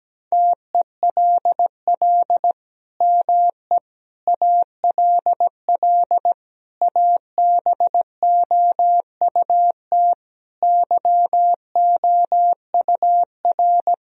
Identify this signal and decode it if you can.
Morse code, heard as a keyed tone, about 17 wpm, 700 Hz. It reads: TELL ME ALL ABOUT YOUR
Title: Sentences from Top 1000 Words - Review Part 4 - 17wpm